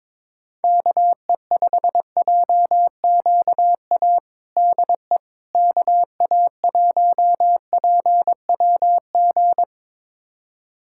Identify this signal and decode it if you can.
Morse code, heard as a keyed tone, about 21 wpm, 700 Hz. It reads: KE5JQA DE KA1PWG
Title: USA Callsign DE Callsign 22wpm